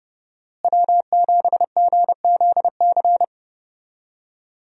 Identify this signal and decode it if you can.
Morse code, heard as a keyed tone, about 32 wpm, 700 Hz. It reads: W7GZC
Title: USA Call Signs - 5 Characters - Repeated 2x - 30wpm